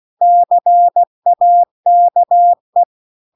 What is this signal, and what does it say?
Morse code, heard as a keyed tone, about 16 wpm, 700 Hz. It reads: CAKE